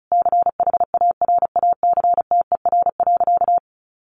Morse code, heard as a keyed tone, about 34 wpm, 700 Hz. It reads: CHARACTER.
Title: Sentences from Top 1200 Words 35wpm